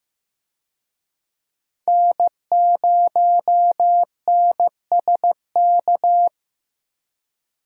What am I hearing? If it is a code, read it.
Morse code, heard as a keyed tone, about 14 wpm, 700 Hz. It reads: N0NSK
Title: USA Call Signs - 5 Characters - Repeated 2x - 15wpm